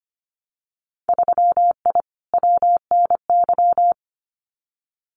Morse code, heard as a keyed tone, about 25 wpm, 700 Hz. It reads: 3I WNY
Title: Winter Field Day Exchanges Encore 25wpm